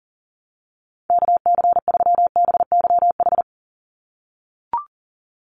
Morse code, heard as a keyed tone, about 40 wpm, 690 Hz. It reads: KC3BYH
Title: USA Call Signs 40wpm